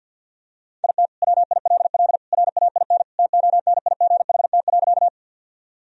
Morse code, heard as a keyed tone, about 49 wpm, 700 Hz. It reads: IT WILL RAIN TONIGHT.
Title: Sentences from Top 1000 Words - Review Part 1 - 50wpm